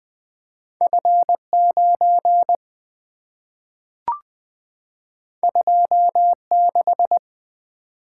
Morse code, heard as a keyed tone, about 20 wpm, 700 Hz. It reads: F 9 2 6
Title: Two Letter-Number Combinations - 20wpm